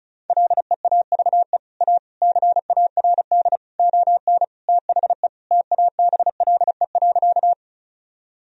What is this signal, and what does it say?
Morse code, heard as a keyed tone, about 34 wpm, 700 Hz. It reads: LEAVE A CARD ON THE TABLE.
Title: Sentences from Top 1000 Words - Review Part 4 - 35wpm